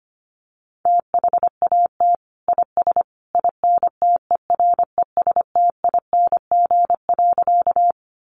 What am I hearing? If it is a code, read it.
Morse code, heard as a keyed tone, about 25 wpm, 700 Hz. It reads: THAT IS INTERESTING.
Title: Sentences from Top 1000 Words 25wpm